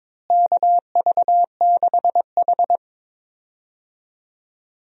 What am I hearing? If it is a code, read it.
Morse code, heard as a keyed tone, about 22 wpm, 700 Hz. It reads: KV6H
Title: USA Call Signs - 4 Characters - Repeated 3x - 22wpm